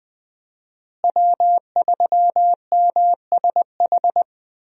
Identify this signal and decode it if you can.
Morse code, heard as a keyed tone, about 20 wpm, 700 Hz. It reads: W3MSH
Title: USA Call Signs - 5 Characters - Repeated 3x - 20wpm